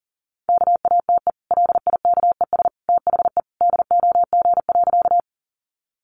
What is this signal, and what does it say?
Morse code, heard as a keyed tone, about 40 wpm, 700 Hz. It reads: KATE LIKES THE DOG.
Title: Easy Phrases 40wpm